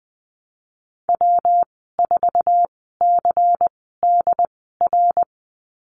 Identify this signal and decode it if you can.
Morse code, heard as a keyed tone, about 20 wpm, spaced as for 15 wpm, 700 Hz. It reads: W4CDR
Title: 140 Lesson 23g - Letter f - Call Signs 20wpm